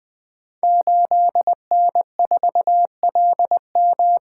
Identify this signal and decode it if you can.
Morse code, heard as a keyed tone, about 20 wpm, 700 Hz. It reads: 8N4LM
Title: Code Groups - 5 Letter:Numbers - 20wpm